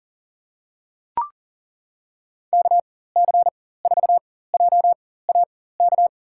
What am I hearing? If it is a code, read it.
Morse code, heard as a keyed tone, about 41 wpm, spaced as for 19 wpm, 700 Hz. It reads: KC4JAK
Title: USA Call Signs - 6 Characters - 4x Character Spacing - 40wpm